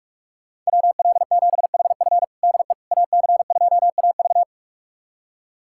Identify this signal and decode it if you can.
Morse code, heard as a keyed tone, about 41 wpm, 700 Hz. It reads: WL7HF DE AC2AV
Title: USA Callsign DE Callsign - 5 Characters - 45wpm